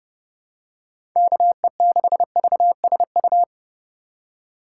Morse code, heard as a keyed tone, about 30 wpm, 700 Hz. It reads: KE6VSU
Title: USA Call Signs - 6 Characters - 2x Character Spacing - 30wpm